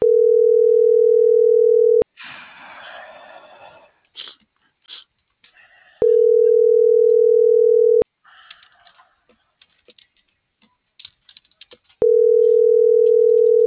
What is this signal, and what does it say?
Ringback tone (North America), 440 + 480 Hz.